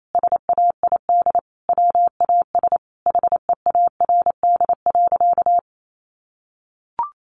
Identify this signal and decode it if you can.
Morse code, heard as a keyed tone, about 28 wpm, 700 Hz. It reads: SAID WAS HEARD.
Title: Sentences from Top 400 Words 28wpm